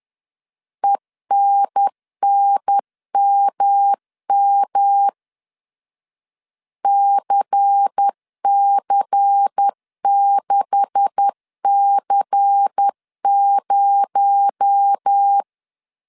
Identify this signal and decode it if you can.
Morse code, heard as a keyed tone, about 10 wpm, 800 Hz. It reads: ENNMM CC6C0
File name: CW / DET / 1 9 1500C S B 10wpm 4439